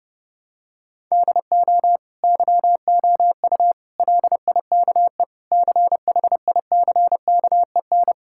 Morse code, heard as a keyed tone, about 30 wpm, 700 Hz. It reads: DO YOU LIKE CHICKEN
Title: Sentences from Top 1000 Words - Review Part 6 - 30wpm